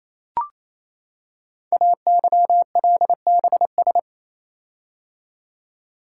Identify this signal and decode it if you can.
Morse code, heard as a keyed tone, about 28 wpm, 700 Hz. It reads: AYLBS
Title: Code Groups - 5 Letter:Numbers - 28wpm